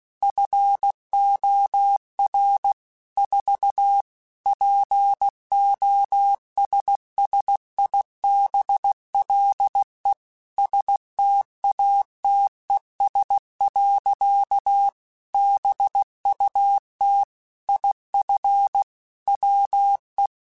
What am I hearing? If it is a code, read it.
Morse code, heard as a keyed tone, about 16 wpm, 800 Hz. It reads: FOR 4 POSSIBLE STATES. BUT IF WE
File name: Technical Subjects / Technical Subjects-mp3s / AND Gate @15